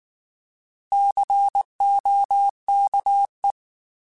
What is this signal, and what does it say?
Morse code, heard as a keyed tone, about 19 wpm, 800 Hz. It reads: COKE
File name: Alphabetical / Alphabetical mp3s / Cs @18